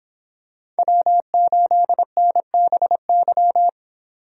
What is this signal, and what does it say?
Morse code, heard as a keyed tone, about 26 wpm, 700 Hz. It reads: W8NBY